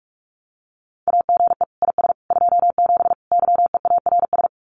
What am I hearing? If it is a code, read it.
Morse code, heard as a keyed tone, about 44 wpm, 700 Hz. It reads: AGE IS 27 YEARS
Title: Common QSO Phrases Encore 45wpm